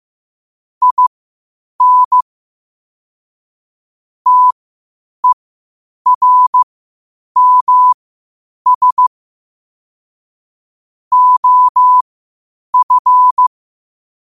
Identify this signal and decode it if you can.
Morse code, heard as a keyed tone, about 15 wpm, spaced as for 8 wpm, 1000 Hz. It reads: IN TERMS OF
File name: qotdfiles / 8wpm